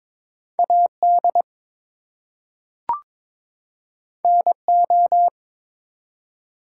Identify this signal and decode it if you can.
Morse code, heard as a keyed tone, about 22 wpm, 700 Hz. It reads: AD NO
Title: Two Letter Words - 22wpm